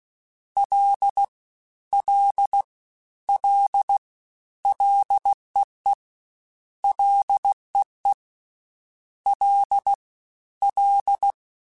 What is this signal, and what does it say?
Morse code, heard as a keyed tone, about 16 wpm, 800 Hz. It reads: L L L LEE LEE L L
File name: Ls / Single L m3s / Single L @15